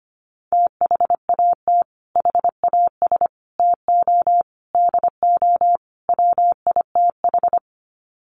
Morse code, heard as a keyed tone, about 25 wpm, 700 Hz. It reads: THAT HAS TO DO WITH
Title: Sets of 5 Words Encore2 25wpm